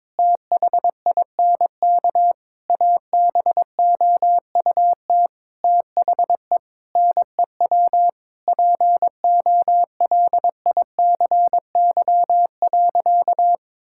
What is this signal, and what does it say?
Morse code, heard as a keyed tone, about 22 wpm, 700 Hz. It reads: THINK ABOUT THE NEW POLICY.
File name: practice / 22wpm / Sentences with 8-Word Phrases - 22wpm